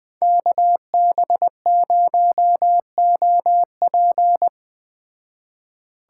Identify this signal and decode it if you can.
Morse code, heard as a keyed tone, about 20 wpm, 700 Hz. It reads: KB0OP
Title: Speed Racing - USA Call Signs 17wpm